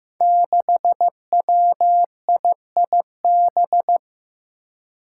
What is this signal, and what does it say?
Morse code, heard as a keyed tone, about 15 wpm, 700 Hz. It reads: 6WIIB
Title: Five Challenging Letter-Number Combinations 15wpm